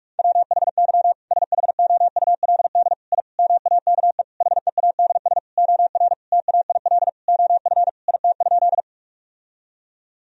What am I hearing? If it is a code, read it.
Morse code, heard as a keyed tone, about 41 wpm, 700 Hz. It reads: WHY SHOULD I MAKE HEADS OR TAIL OF IT?
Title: Sentences from Top 1000 Words - Review Part 3 - 45wpm